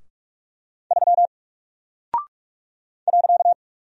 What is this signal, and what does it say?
Morse code, heard as a keyed tone, about 44 wpm, 700 Hz. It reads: 3 .
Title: Single Character (Letter:Number:Punctuation:Prosign) Alt 45wpm